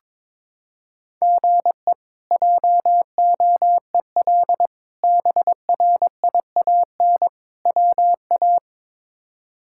Morse code, heard as a keyed tone, about 22 wpm, 700 Hz. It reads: GE JOEL BRIAN WA
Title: K1USN SST QSO Practice - 22wpm